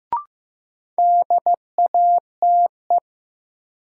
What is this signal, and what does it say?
Morse code, heard as a keyed tone, about 15 wpm, 700 Hz. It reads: DATE
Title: Sentences from Top 1000 Words - Review Part 7 - New Word Drill - 15wpm